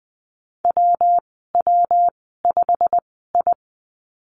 Morse code, heard as a keyed tone, about 20 wpm, spaced as for 15 wpm, 700 Hz. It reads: WW5I
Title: 161 Lesson 26g - Letter g - Call Signs 20wpm